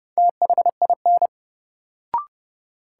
Morse code, heard as a keyed tone, about 30 wpm, 700 Hz. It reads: THIN
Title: Sentences from Top 1200 Words - New Word Drill - 30wpm